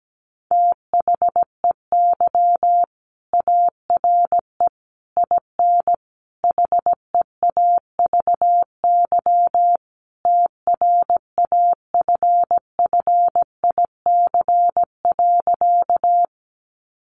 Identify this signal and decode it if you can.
Morse code, heard as a keyed tone, about 17 wpm, 700 Hz. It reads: THEY ARE IN HEAVY TRAFFIC.
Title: Sentences from Top 1300 Words 17wpm